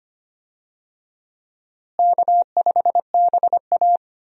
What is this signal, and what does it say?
Morse code, heard as a keyed tone, about 25 wpm, 700 Hz. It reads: K5BA
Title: USA Call Signs - 4 Characters - Repeated 2x - 25wpm